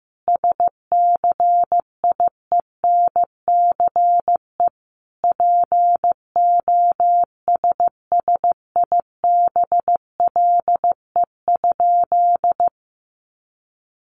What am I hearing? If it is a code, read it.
Morse code, heard as a keyed tone, about 15 wpm, 700 Hz. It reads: SCIENCE POSSIBLE?
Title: Sentences from Top 1500 Words 15wpm